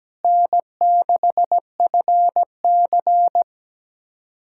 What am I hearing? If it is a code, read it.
Morse code, heard as a keyed tone, about 17 wpm, 700 Hz. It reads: N6FC
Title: USA Callsign DE Callsign - 4 Characters - 17wpm